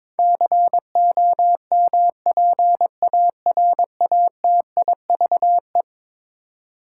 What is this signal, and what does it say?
Morse code, heard as a keyed tone, about 22 wpm, 700 Hz. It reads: COMPARATIVE
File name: practice / 22wpm / Eleven Letter Words - 22wpm